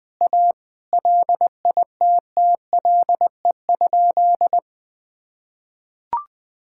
Morse code, heard as a keyed tone, about 20 wpm, 700 Hz. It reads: A LITTLE?
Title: Sentences from Top 1600 Words 20wpm